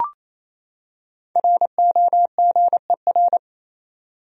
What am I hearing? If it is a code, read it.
Morse code, heard as a keyed tone, about 27 wpm, 700 Hz. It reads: ROGER
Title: USA Common Names - Max 5 Letters - 28wpm